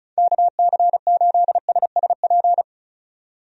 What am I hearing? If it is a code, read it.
Morse code, heard as a keyed tone, about 34 wpm, 700 Hz. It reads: KC8SSP
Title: USA Call Signs - 6 Characters - 35wpm